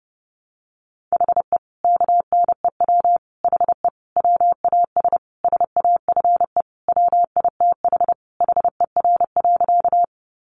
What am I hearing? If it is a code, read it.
Morse code, heard as a keyed tone, about 28 wpm, 700 Hz. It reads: HE KNEW HE WAS SAFE WITH HER.
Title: Sentences from Top 900 Words 30wpm